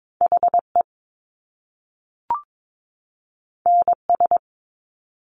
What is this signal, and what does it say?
Morse code, heard as a keyed tone, about 22 wpm, 700 Hz. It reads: HE NS
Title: Top Two Letter Combinations in Words 22wpm